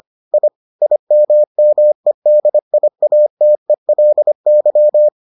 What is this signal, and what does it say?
Morse code, heard as a keyed tone, about 25 wpm, 600 Hz. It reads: I IMMEDIATELY